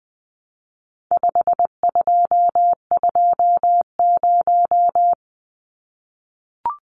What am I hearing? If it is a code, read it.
Morse code, heard as a keyed tone, about 20 wpm, 700 Hz. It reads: 5220
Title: Numbers - 4-Digit 20wpm